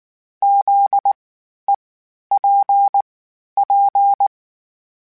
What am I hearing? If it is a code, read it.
Morse code, heard as a keyed tone, about 19 wpm, spaced as for 11 wpm, 800 Hz. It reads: ZEPP
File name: (spaced) / FromSMs spaced mp3s / Antennas sp@18